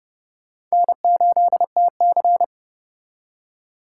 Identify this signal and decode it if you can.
Morse code, heard as a keyed tone, about 36 wpm, 700 Hz. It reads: N8TC
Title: Speed Racing - USA 1x2 Call Signs 30wpm